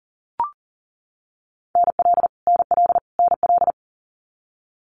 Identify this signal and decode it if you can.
Morse code, heard as a keyed tone, about 40 wpm, 700 Hz. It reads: NL NL NL
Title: Canadian Provinces and Territories - Repeated 3x - 40wpm